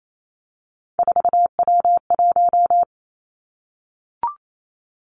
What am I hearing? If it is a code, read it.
Morse code, heard as a keyed tone, about 28 wpm, 700 Hz. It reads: 4W1
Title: Three Challenging Letter-Number Combinations 28wpm